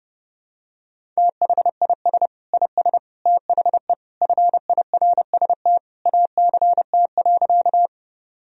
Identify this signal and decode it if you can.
Morse code, heard as a keyed tone, about 32 wpm, 700 Hz. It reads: THIS IS THE FIRST ACT.
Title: Sentences from Top 1000 Words - Review Part 1 - 30wpm